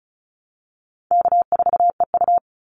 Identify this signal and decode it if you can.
Morse code, heard as a keyed tone, about 34 wpm, 700 Hz. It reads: K4EU
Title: Call Signs - 35wpm